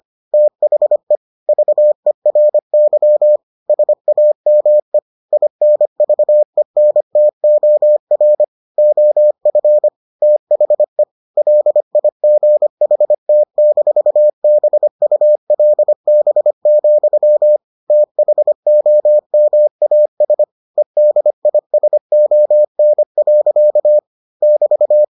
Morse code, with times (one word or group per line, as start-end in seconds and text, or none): 0.3-1.2 THE
1.5-3.4 VERY
3.7-5.0 SAME
5.3-8.4 INVENTOR
8.8-9.9 OF
10.2-11.0 THE
11.4-17.6 LIGHT-BULB,
17.9-20.4 THOMAS
20.8-24.0 EDISON.
24.4-25.0 <BT>